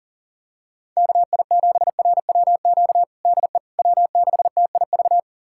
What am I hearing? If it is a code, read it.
Morse code, heard as a keyed tone, about 40 wpm, 700 Hz. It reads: KI7RWQ DE W6TIV
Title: USA Callsign DE Callsign 40wpm